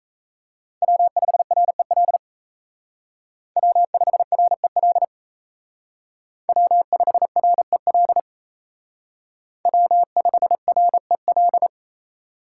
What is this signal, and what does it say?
Morse code, heard as a keyed tone, about 32 wpm, 700 Hz. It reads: W5REL W5REL W5REL W5REL
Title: Speed Racing - USA 1x3 Call Signs 28wpm